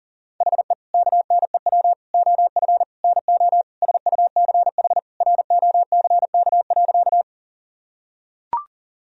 Morse code, heard as a keyed tone, about 40 wpm, 700 Hz. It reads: HE KNEW OF NO SUCH ROCK.
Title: Sentences from Top 1000 Words - Review Part 2 - 40wpm